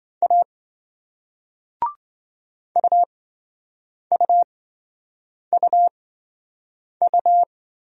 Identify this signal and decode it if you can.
Morse code, heard as a keyed tone, about 26 wpm, spaced as for 7 wpm, 700 Hz. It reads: A UUUU